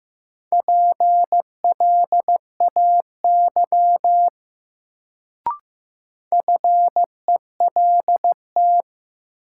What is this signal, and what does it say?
Morse code, heard as a keyed tone, about 15 wpm, 700 Hz. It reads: PLAY FELT
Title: Sets of 5 Words Encore - New Word Drill - 15wpm